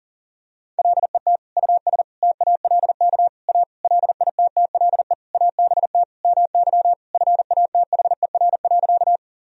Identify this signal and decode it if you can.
Morse code, heard as a keyed tone, about 40 wpm, 700 Hz. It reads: LET US TALK A LITTLE ABT MY FATHER.
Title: Sentences from Top 400 Words and Abbreviations 40wpm